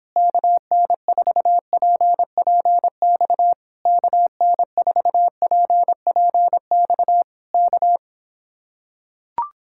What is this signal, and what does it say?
Morse code, heard as a keyed tone, about 26 wpm, 700 Hz. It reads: KN4PPX KN4PPX K